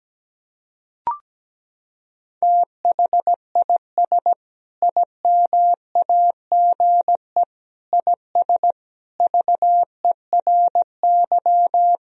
Morse code, heard as a keyed tone, about 17 wpm, 700 Hz. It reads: THIS IMAGE IS VERY
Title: Sentences from Top 1000 Words - Review Part 4 - 17wpm